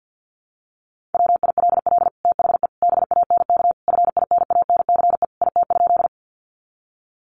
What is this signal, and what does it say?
Morse code, heard as a keyed tone, about 50 wpm, 700 Hz. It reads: WILL THE BANK FINANCE IT?
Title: Sentences from Top 1500 Words 50wpm